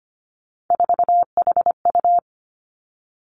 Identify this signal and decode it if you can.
Morse code, heard as a keyed tone, about 25 wpm, 700 Hz. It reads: 4HU
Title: Three Challenging Letter-Number Combinations 25wpm